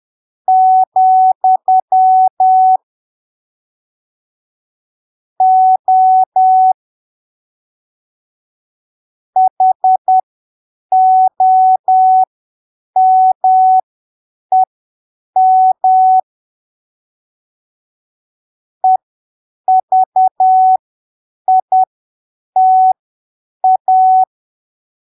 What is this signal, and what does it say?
Morse code, heard as a keyed tone, about 10 wpm, spaced as for 7 wpm, 750 Hz. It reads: , O HOMEM EVITA
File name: arquivos / texto / texto 23 10ppm